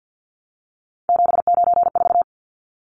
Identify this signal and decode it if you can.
Morse code, heard as a keyed tone, about 50 wpm, 700 Hz. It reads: 794
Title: Numbers - 3-Digit 50wpm